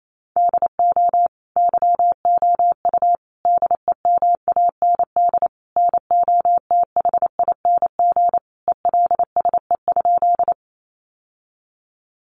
Morse code, heard as a keyed tone, about 28 wpm, 700 Hz. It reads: DO YOU DEMAND NOTHING ELSE?
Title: Sentences from Top 800 Words 28wpm